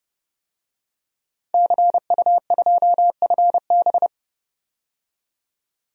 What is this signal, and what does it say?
Morse code, heard as a keyed tone, about 30 wpm, 700 Hz. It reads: CU2FB
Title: DXCC-100 Call Signs - 5 Characters - 30wpm